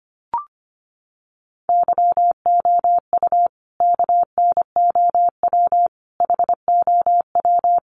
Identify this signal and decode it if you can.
Morse code, heard as a keyed tone, about 25 wpm, 700 Hz. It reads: YOU KNOW HOW
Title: Sentences from Top 1100 Words 25wpm